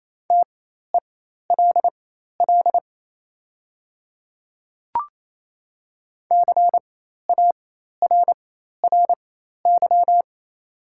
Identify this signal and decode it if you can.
Morse code, heard as a keyed tone, about 28 wpm, spaced as for 13 wpm, 700 Hz. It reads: TELL CARRY
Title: 4x Character Spacing - Top 300 Words - 28wpm